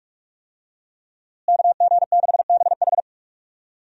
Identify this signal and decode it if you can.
Morse code, heard as a keyed tone, about 43 wpm, 700 Hz. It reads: KG6BH
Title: USA Call Signs - 5 Characters - Repeated 2x - 45wpm